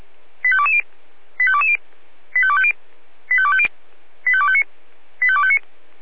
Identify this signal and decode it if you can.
Five-tone selective call (ZVEI), heard as quick runs of five tones, tones of about 70 ms. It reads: 85220 85220 85250 85250 85259 85259